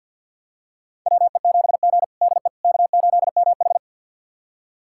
Morse code, heard as a keyed tone, about 48 wpm, 700 Hz. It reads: WE7G DE K8MH
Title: USA Callsign DE Callsign - 4 Characters - 50wpm